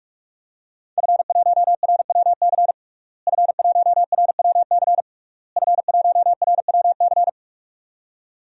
Morse code, heard as a keyed tone, about 46 wpm, 700 Hz. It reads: F1RWC F1RWC F1RWC
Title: DXCC-100 Call Signs - 5 Characters - Repeated 3x - 45wpm